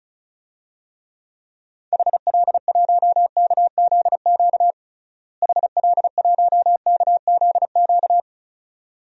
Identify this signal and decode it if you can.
Morse code, heard as a keyed tone, about 36 wpm, 700 Hz. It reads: HL1KZQ HL1KZQ
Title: DXCC-100 Call Signs - 6 Characters - Repeated 2x - 35wpm